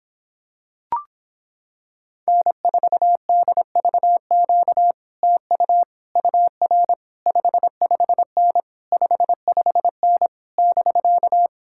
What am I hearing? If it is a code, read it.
Morse code, heard as a keyed tone, about 26 wpm, 700 Hz. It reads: N4DVQ TU UR 55N 55N <BK>